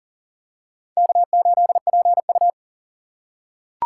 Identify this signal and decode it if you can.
Morse code, heard as a keyed tone, about 40 wpm, 700 Hz. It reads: K8PU
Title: USA Call Signs - 2x Character Spacing - 40wpm